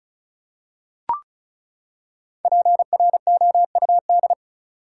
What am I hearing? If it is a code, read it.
Morse code, heard as a keyed tone, about 36 wpm, 700 Hz. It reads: PROUD
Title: Five Letter Words - 35wpm